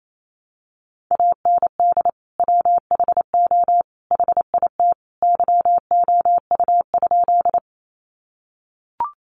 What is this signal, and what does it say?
Morse code, heard as a keyed tone, about 28 wpm, 700 Hz. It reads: AND WHO HIT YOU?
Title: Sentences from Top 1200 Words 28wpm